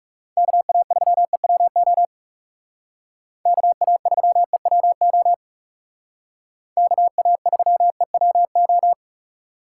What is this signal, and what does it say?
Morse code, heard as a keyed tone, about 38 wpm, 700 Hz. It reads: KA3EWO KA3EWO KA3EWO